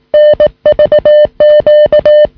Morse code, heard as a keyed tone, about 18 wpm, 600 Hz. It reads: NVQ
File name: sm7lcb / 2104 oh6nvq 13cm